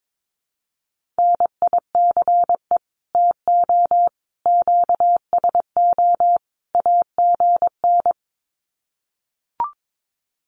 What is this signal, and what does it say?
Morse code, heard as a keyed tone, about 22 wpm, 700 Hz. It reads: NICE TO QSO AGN
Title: Common QSO Phrases 22wpm